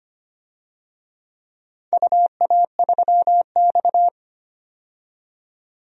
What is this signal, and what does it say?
Morse code, heard as a keyed tone, about 25 wpm, 690 Hz. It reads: UA3X
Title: DXCC-100 Call Signs - 4 Characters - 25wpm